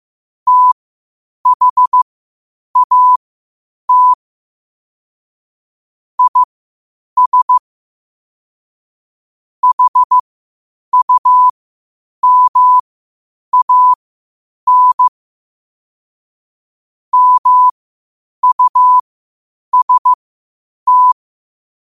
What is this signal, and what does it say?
Morse code, heard as a keyed tone, about 15 wpm, spaced as for 8 wpm, 1000 Hz. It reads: THAT IS HUMAN MUST
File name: qotdfiles / 8wpm